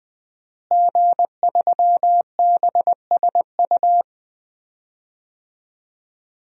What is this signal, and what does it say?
Morse code, heard as a keyed tone, about 20 wpm, 700 Hz. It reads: G3BSU
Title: DXCC-100 Call Signs - 5 Characters - Repeated 3x - 20wpm